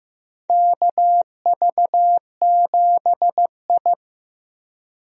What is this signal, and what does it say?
Morse code, heard as a keyed tone, about 15 wpm, 700 Hz. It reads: KV7I
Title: USA Call Signs - 4 Characters - 15wpm